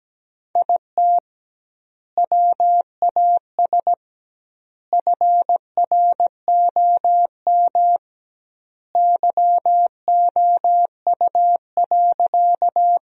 Morse code, heard as a keyed tone, about 17 wpm, 700 Hz. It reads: IT WAS FROM YOU.
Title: 2x Word-Spacing - Sentences from Top 100 Words - 17wpm